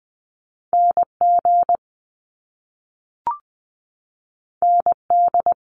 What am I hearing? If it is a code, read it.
Morse code, heard as a keyed tone, about 20 wpm, 700 Hz. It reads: NG ND
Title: Top Two Letter Combinations in Words 20wpm